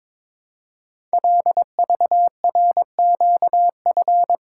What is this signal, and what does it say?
Morse code, heard as a keyed tone, about 22 wpm, 700 Hz. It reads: LVRQF
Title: Code Groups - 5 Letters - 22wpm